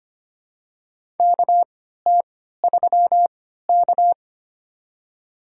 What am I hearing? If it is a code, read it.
Morse code, heard as a keyed tone, about 25 wpm, spaced as for 14 wpm, 700 Hz. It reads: KT3K
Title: USA Call Signs - 4 Characters - 3x Character Spacing - 25wpm